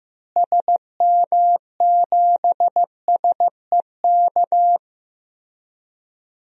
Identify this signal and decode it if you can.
Morse code, heard as a keyed tone, about 15 wpm, 700 Hz. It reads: SM7SEK
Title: DXCC-100 Call Signs - 6 Characters - Repeated 3x - 15wpm